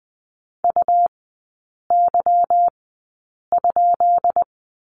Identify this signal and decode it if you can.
Morse code, heard as a keyed tone, about 20 wpm, spaced as for 8 wpm, 690 Hz. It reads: UY?
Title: 145 Lesson 24e - Letter y - Three Characters 20wpm